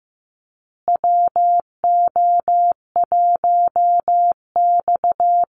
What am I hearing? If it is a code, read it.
Morse code, heard as a keyed tone, about 15 wpm, 700 Hz. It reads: WO1X
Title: Call Signs Repeated 2x - 15wpm